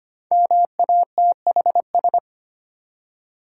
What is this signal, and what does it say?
Morse code, heard as a keyed tone, about 25 wpm, 700 Hz. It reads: MATHS
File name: 5 Letter Words - First 3 Letters Given - 25wpm